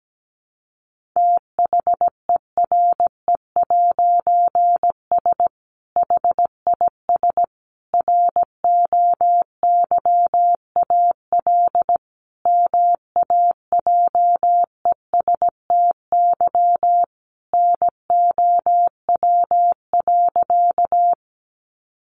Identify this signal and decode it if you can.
Morse code, heard as a keyed tone, about 17 wpm, 700 Hz. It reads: THERE'S HIS ROYAL MAJESTY NOW.
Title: A Sound Of Thunder 17wpm